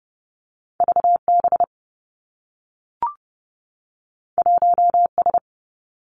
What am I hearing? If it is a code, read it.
Morse code, heard as a keyed tone, about 30 wpm, 700 Hz. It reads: V B 1 S
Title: Two Challenging Letter-Number Combinations 30wpm